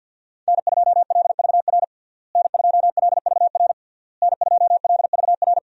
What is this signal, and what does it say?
Morse code, heard as a keyed tone, about 52 wpm, 700 Hz. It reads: N2LVR N2LVR N2LVR
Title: USA Call Signs - 5 Characters - Repeated 3x - 50wpm